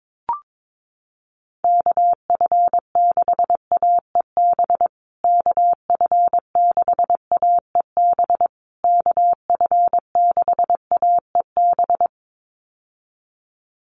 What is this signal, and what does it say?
Morse code, heard as a keyed tone, about 22 wpm, 700 Hz. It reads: KF6AEB KF6AEB KF6AEB
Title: USA Call Signs Repeated 3x 22wpm